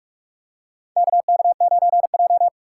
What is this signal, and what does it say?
Morse code, heard as a keyed tone, about 46 wpm, 690 Hz. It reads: KK9J
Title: USA Call Signs - 4 Characters - 8x Character Spacing - 45wpm